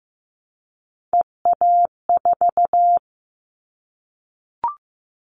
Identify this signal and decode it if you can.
Morse code, heard as a keyed tone, about 15 wpm, 700 Hz. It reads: EA4
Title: Callsign Prefixes for Top 25 DXCC - Repeated 3x - 15wpm